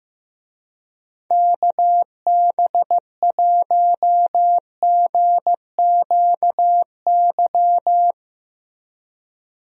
Morse code, heard as a keyed tone, about 16 wpm, 700 Hz. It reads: KB1GQY
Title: USA Call Signs - 6 Characters - Repeated 3x - 15wpm